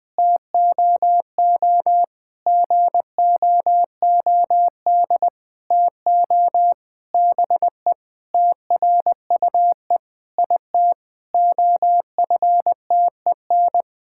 Morse code, heard as a keyed tone, about 20 wpm, 700 Hz. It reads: TOO GOOD TO BE TRUE IT OFTEN
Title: Sentences with 9-Word Phrases - 20wpm